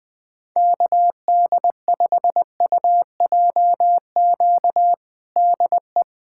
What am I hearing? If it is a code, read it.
Morse code, heard as a keyed tone, about 20 wpm, 700 Hz. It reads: KD5UJQ DE
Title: USA Callsign DE Callsign - 6 Characters - 20wpm